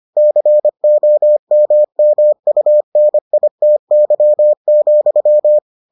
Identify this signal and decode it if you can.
Morse code, heard as a keyed tone, about 25 wpm, 600 Hz. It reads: COMMUNITY,